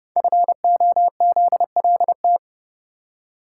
Morse code, heard as a keyed tone, about 30 wpm, 700 Hz. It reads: FOZLT
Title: Code Groups - 5 Letters - 30wpm